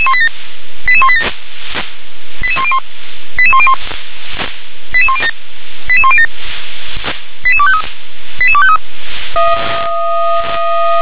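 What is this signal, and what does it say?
Five-tone selective call (ZVEI), heard as quick runs of five tones, tones of about 70 ms. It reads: 88187 88101 88101 88107 88107 88253 88253
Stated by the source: ZVEI five-tone selective call 88187 received with an SDR.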